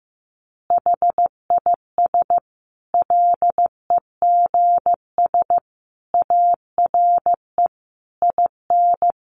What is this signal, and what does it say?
Morse code, heard as a keyed tone, about 14 wpm, 700 Hz. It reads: HIS LEGS ARE IN
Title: Additional Easy Phrases 15wpm